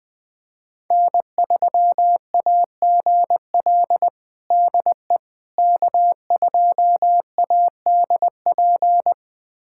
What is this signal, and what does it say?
Morse code, heard as a keyed tone, about 20 wpm, 700 Hz. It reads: N3AGL DE K2ADP
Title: USA Callsign DE Callsign - 5 Characters - 20wpm